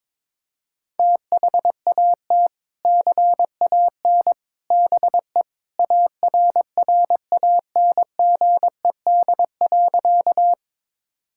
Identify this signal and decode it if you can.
Morse code, heard as a keyed tone, about 22 wpm, 700 Hz. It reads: THAT CAN BE ARRANGED.
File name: practice / 22wpm / Sentences with 3-Word Phrases - 22wpm